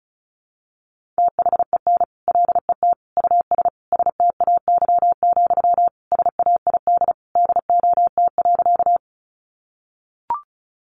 Morse code, heard as a keyed tone, about 34 wpm, 700 Hz. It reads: THEN LET US STAY, SAID DOT.
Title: Sentences from Top 1500 Words 35wpm